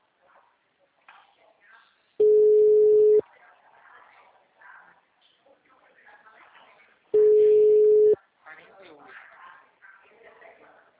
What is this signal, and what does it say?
Ringback tone (Europe, 425 Hz), with speech.